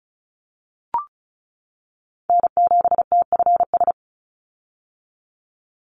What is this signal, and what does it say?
Morse code, heard as a keyed tone, about 34 wpm, 700 Hz. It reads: N7TFS
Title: USA Call Signs 35wpm